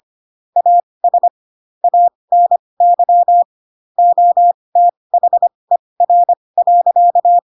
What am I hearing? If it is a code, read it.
Morse code, heard as a keyed tone, about 25 wpm, spaced as for 20 wpm, 700 Hz. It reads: AS ANY OTHER.
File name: Pinocchio / Pinocchio ch1 15 25 ik1hge